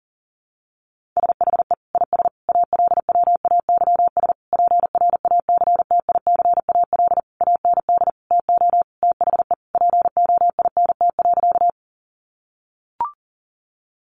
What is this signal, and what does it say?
Morse code, heard as a keyed tone, about 40 wpm, 700 Hz. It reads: SHE IS ALWAYS PRACTICAL AND TO THE POINT.
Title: Sentences from Top 1400 Words 40wpm